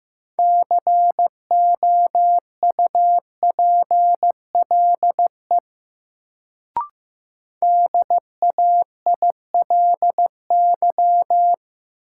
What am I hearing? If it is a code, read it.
Morse code, heard as a keyed tone, about 15 wpm, 700 Hz. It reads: COUPLE DAILY
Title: Sentences from Top 1100 Words - New Word Drill - 15wpm